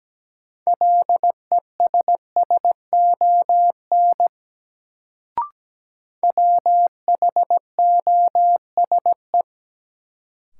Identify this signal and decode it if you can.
Morse code, heard as a keyed tone, about 17 wpm, 700 Hz. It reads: LESSON WHOSE
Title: Sentences from Top 1200 Words - New Word Drill Alt - 17wpm